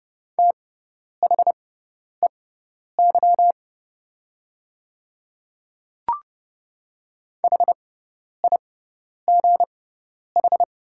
Morse code, heard as a keyed tone, about 30 wpm, spaced as for 10 wpm, 700 Hz. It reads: THEY HIGH
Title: 6x Character Spacing - Top 200 Words - 30wpm